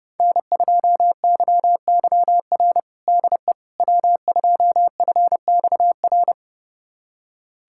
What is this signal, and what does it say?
Morse code, heard as a keyed tone, about 30 wpm, 700 Hz. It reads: N2YYR DE W2FXR
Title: USA Callsign DE Callsign - 5 Characters - 30wpm